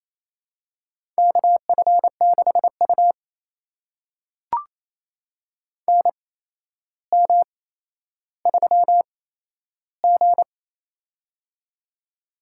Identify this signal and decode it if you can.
Morse code, heard as a keyed tone, about 28 wpm, spaced as for 12 wpm, 700 Hz. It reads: KF6U NM3G
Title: USA Call Signs - 4 Characters - 8x Character Spacing - 28wpm